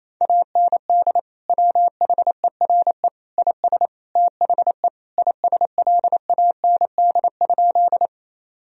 Morse code, heard as a keyed tone, about 28 wpm, 700 Hz. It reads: AND WHERE IS THE ISLAND?
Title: Sentences from Top 1000 Words - Review Part 3 - 28wpm